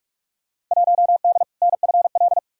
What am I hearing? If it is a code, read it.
Morse code, heard as a keyed tone, about 46 wpm, 700 Hz. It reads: 1D NFL
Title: Field Day Exchanges - 45wpm